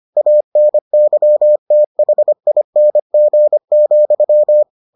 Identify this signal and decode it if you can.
Morse code, heard as a keyed tone, about 25 wpm, 600 Hz. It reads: ANYTHING,